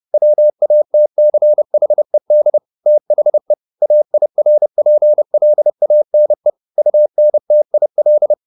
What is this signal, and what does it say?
Morse code, heard as a keyed tone, about 30 wpm, 600 Hz. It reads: WATCHED THE AIRPLANE UNTIL